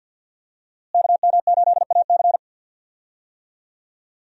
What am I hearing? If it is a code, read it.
Morse code, heard as a keyed tone, about 51 wpm, 700 Hz. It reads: KM8AC
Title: USA Call Signs - 5 Characters - Repeated 3x - 50wpm